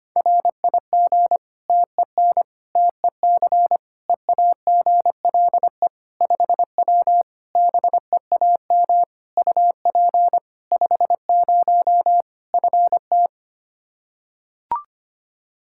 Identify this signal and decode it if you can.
Morse code, heard as a keyed tone, about 25 wpm, 700 Hz. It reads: RIG TEN TEC EAGLE 5W BEAM UP 50 FT
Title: Practice QSO - Exchange 2 - Info - 25wpm